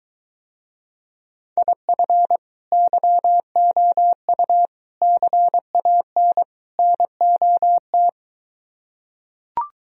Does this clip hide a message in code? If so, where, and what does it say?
Yes: Morse code from 1.6 to 8.1 s, heard as a keyed tone, about 23 wpm, 700 Hz. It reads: IF YOU CAN NOT